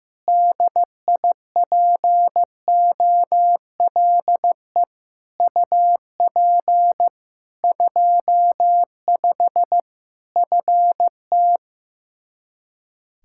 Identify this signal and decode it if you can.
Morse code, heard as a keyed tone, about 15 wpm, 700 Hz. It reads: DIPOLE UP 25 FT
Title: Practice QSO - Exchange 2 - Info - 15wpm